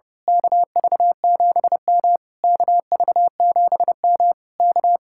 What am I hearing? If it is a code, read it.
Morse code, heard as a keyed tone, about 28 wpm, 700 Hz. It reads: KV7M KV7M K
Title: Speed-Racing Common QSO Phrases - 30wpm